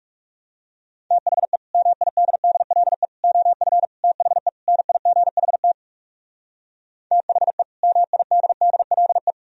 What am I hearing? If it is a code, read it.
Morse code, heard as a keyed tone, about 38 wpm, 700 Hz. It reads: THE MIDDLE OF THE NIGHT THE MIDDLE